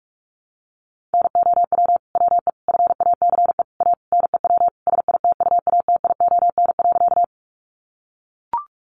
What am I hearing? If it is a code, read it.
Morse code, heard as a keyed tone, about 44 wpm, 700 Hz. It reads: NOW WE FACE A NEW SITUATION.
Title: Sentences from Top 1200 Words 45wpm